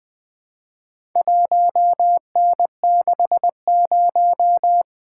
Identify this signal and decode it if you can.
Morse code, heard as a keyed tone, about 20 wpm, 700 Hz. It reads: 1N60
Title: Four Letter-Number Combinations - 20wpm